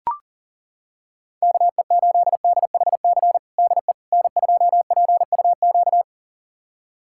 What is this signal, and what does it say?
Morse code, heard as a keyed tone, about 40 wpm, 700 Hz. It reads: KE8DHC DE N2PUQ
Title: USA Callsign DE Callsign 40wpm